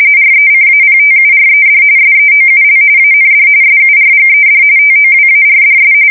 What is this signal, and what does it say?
RTTY, about 45.45 baud. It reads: RECAPTURED SEOUL AND SURGED TOWARDS
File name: fsk170 / battleshipnj